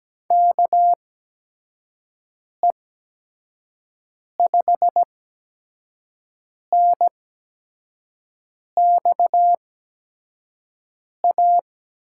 Morse code, heard as a keyed tone, about 17 wpm, spaced as for 5 wpm, 700 Hz. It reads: KE5NXA
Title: USA Call Signs - 6 Characters - 8x Character Spacing - 17wpm